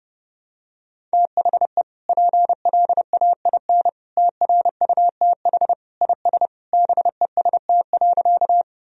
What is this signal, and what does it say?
Morse code, heard as a keyed tone, about 30 wpm, 700 Hz. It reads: THE PLAIN TRUTH IS BEST.
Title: Sentences from Top 1000 Words - Review Part 2 - 30wpm